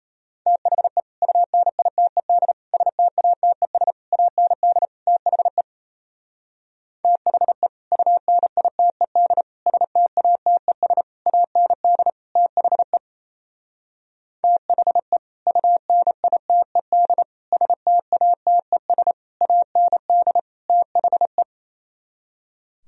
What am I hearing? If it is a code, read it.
Morse code, heard as a keyed tone, about 32 wpm, 700 Hz. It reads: THE UNITED STATES AND THE THE UNITED STATES AND THE THE UNITED STATES AND THE
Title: Speed-Racing - Sets of 5 Words 28wpm